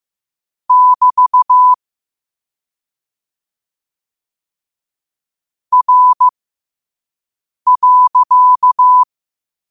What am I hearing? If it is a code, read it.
Morse code, heard as a keyed tone, about 15 wpm, spaced as for 5 wpm, 1000 Hz. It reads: <BT> R.
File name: qotdfiles / 5wpm